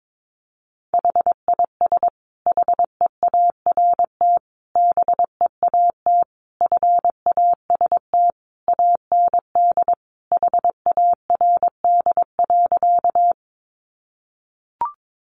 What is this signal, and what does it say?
Morse code, heard as a keyed tone, about 22 wpm, 700 Hz. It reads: HIS HEART BEAT FAST AND HARD.
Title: Sentences from Top 1300 Words 22wpm